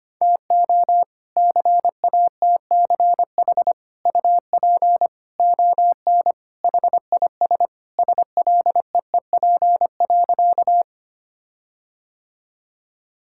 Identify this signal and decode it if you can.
Morse code, heard as a keyed tone, about 25 wpm, 700 Hz. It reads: TO CATCH UP ON HIS SLEEP.
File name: practice / 25wpm / All-Age Appropriate Jokes - 25wpm